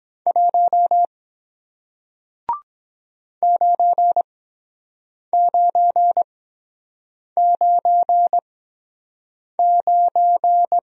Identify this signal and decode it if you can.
Morse code, heard as a keyed tone, about 23 wpm, spaced as for 7 wpm, 700 Hz. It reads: 1 9999